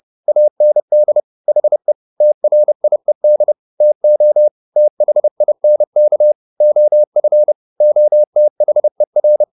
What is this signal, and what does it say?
Morse code, heard as a keyed tone, about 30 wpm, 600 Hz. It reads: AND HE TRIED TO THINK OF OTHER
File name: the old man and the sea / the old man and the sea 1 30wpm